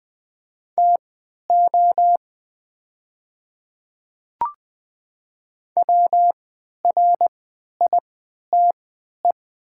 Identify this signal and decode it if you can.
Morse code, heard as a keyed tone, about 20 wpm, spaced as for 11 wpm, 700 Hz. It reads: TO WRITE
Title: 3x Character Spacing - Top 100 Words - 20wpm